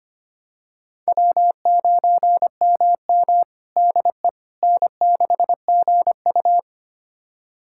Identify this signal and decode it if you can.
Morse code, heard as a keyed tone, about 25 wpm, 700 Hz. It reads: W9MM DE N6GU
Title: USA Callsign DE Callsign - 4 Characters - 25wpm